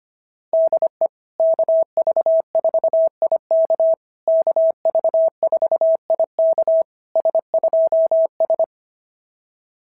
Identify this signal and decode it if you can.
Morse code, heard as a keyed tone, about 24 wpm, 650 Hz. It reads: DE KV4IK KV4IK S2S
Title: SOTA QSO Practice 25wpm